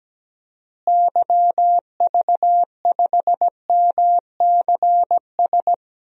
Morse code, heard as a keyed tone, about 17 wpm, 700 Hz. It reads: YV5MCS
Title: DXCC-100 Call Signs - 6 Characters - Repeated 2x - 17wpm